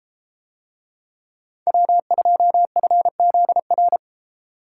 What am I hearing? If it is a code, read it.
Morse code, heard as a keyed tone, about 32 wpm, 700 Hz. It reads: W2FZR